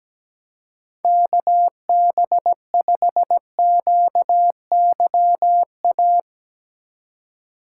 Morse code, heard as a keyed tone, about 17 wpm, 700 Hz. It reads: KB5QYA
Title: USA Call Signs - 6 Characters - 3x Character Spacing - 17wpm